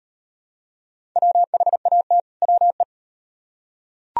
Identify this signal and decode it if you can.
Morse code, heard as a keyed tone, about 35 wpm, 700 Hz. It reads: WHAT WE